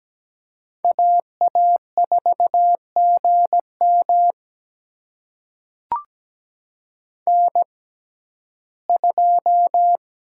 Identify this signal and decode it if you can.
Morse code, heard as a keyed tone, about 17 wpm, spaced as for 12 wpm, 700 Hz. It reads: AA4GM N2
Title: USA Call Signs - 5 Characters - 6x Character Spacing - 17wpm